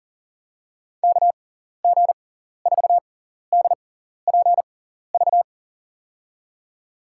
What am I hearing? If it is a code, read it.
Morse code, heard as a keyed tone, about 40 wpm, spaced as for 14 wpm, 700 Hz. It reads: KG4DPV
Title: USA Call Signs - 6x Character Spacing - 40wpm